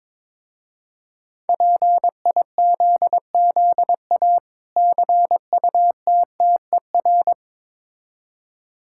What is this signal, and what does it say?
Morse code, heard as a keyed tone, about 22 wpm, 700 Hz. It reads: PIZZA CUTTER
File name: Difficult Words with Hints - 22wpm